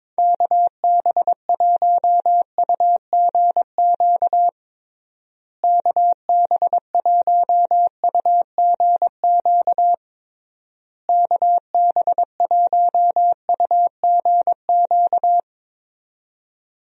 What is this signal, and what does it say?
Morse code, heard as a keyed tone, about 22 wpm, 700 Hz. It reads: KB1UGQ KB1UGQ KB1UGQ
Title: USA Call Signs - 6 Characters - Repeated 3x - 22wpm